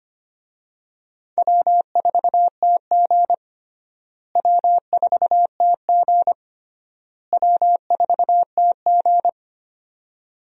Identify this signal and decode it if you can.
Morse code, heard as a keyed tone, about 25 wpm, 700 Hz. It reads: W4TG W4TG W4TG
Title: USA Call Signs - 4 Characters - Repeated 3x - 25wpm